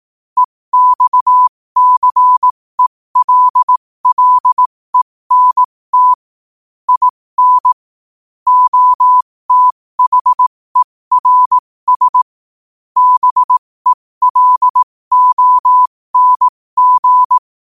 Morse code, heard as a keyed tone, about 18 wpm, spaced as for 15 wpm, 1000 Hz. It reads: EXCELLENT IN OTHERS BELONG
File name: qotdfiles / 15wpm